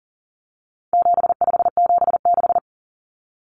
Morse code, heard as a keyed tone, about 40 wpm, 700 Hz. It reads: 7576
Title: Common Cut Numbers - 4-Digit 40wpm